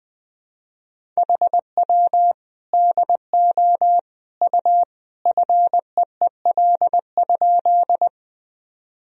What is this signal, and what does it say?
Morse code, heard as a keyed tone, about 19 wpm, 700 Hz. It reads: HW DO U FEEL?
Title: Sentences from Top 400 Words and Abbreviations 20wpm